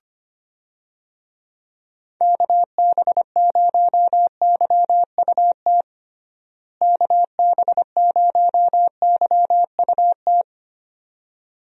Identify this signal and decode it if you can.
Morse code, heard as a keyed tone, about 25 wpm, 700 Hz. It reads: KB0YUT KB0YUT
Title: USA Call Signs - 6 Characters - Repeated 2x - 25wpm